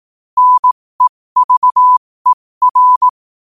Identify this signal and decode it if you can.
Morse code, heard as a keyed tone, about 18 wpm, spaced as for 15 wpm, 1000 Hz. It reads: NEVER
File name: qotdfiles / 15wpm